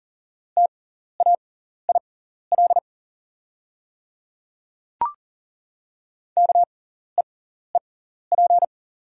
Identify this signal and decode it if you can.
Morse code, heard as a keyed tone, about 38 wpm, spaced as for 14 wpm, 700 Hz. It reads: TAIL KEEP
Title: 6x Character Spacing - Top 500 Words - 40wpm